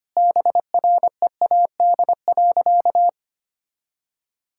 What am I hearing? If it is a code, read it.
Morse code, heard as a keyed tone, about 25 wpm, 700 Hz. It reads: BREAD.
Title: Sentences from Top 1600 Words 25wpm